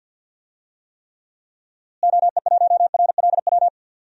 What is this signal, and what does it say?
Morse code, heard as a keyed tone, about 51 wpm, 700 Hz. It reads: OE1RLW
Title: DXCC-100 Call Signs - 6 Characters - Repeated 3x - 50wpm